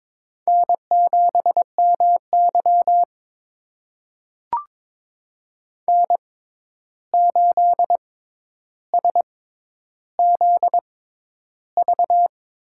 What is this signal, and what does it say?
Morse code, heard as a keyed tone, about 22 wpm, spaced as for 11 wpm, 700 Hz. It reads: N7MY N8SZV
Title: USA Call Signs - 6x Character Spacing - 22wpm